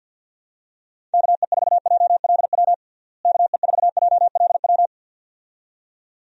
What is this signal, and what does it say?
Morse code, heard as a keyed tone, about 48 wpm, 700 Hz. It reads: KE4JLW KE4JLW
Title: USA Call Signs - 6 Characters - Repeated 2x - 50wpm